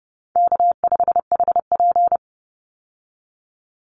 Morse code, heard as a keyed tone, about 30 wpm, 700 Hz. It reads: K5HP
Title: Easy Call Signs Repeated 3x - 30wpm